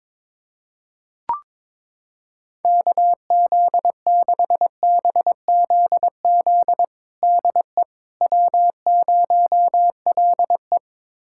Morse code, heard as a keyed tone, about 22 wpm, 700 Hz. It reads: KZ6BZZ DE W0LE
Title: USA Callsign DE Callsign 22wpm